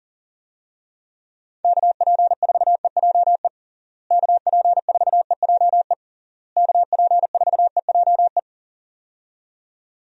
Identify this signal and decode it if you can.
Morse code, heard as a keyed tone, about 40 wpm, 700 Hz. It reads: KP4EJE KP4EJE KP4EJE
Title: DXCC-100 Call Signs - 6 Characters - Repeated 3x - 40wpm